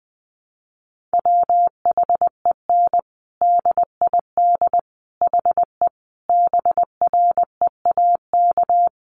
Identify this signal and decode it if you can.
Morse code, heard as a keyed tone, about 19 wpm, 700 Hz. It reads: WHEN DID HE BREAK
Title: Sentences from Top 900 Words 20wpm